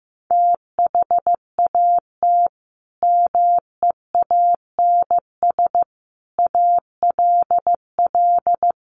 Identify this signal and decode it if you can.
Morse code, heard as a keyed tone, about 15 wpm, 700 Hz. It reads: THAT MEANS ALL
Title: A Sound Of Thunder 15wpm